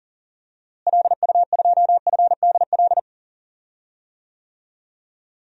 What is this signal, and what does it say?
Morse code, heard as a keyed tone, about 37 wpm, 700 Hz. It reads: LU2FDL
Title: DXCC-100 Call Signs - 6 Characters - Repeated 3x - 40wpm